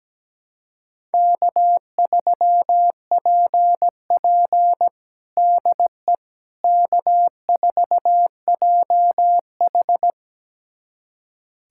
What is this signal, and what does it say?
Morse code, heard as a keyed tone, about 17 wpm, 700 Hz. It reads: K3PP DE K4JH
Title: USA Callsign DE Callsign - 4 Characters - 17wpm